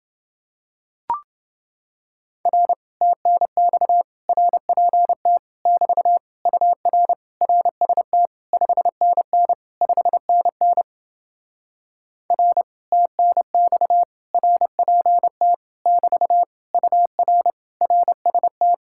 Morse code, heard as a keyed tone, about 30 wpm, 700 Hz. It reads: R TNX RPT <BT> UR RST 5NN 5NN R TNX RPT <BT> UR RST